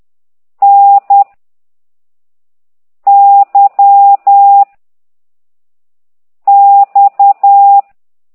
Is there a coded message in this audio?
Yes: Morse code from 0.6 to 7.8 s, heard as a keyed tone, about 10 wpm, 800 Hz. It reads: N Y X